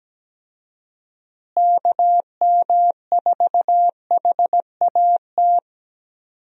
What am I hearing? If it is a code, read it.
Morse code, heard as a keyed tone, about 17 wpm, 700 Hz. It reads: KM4HAT
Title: USA Call Signs - 6 Characters - Repeated 2x - 17wpm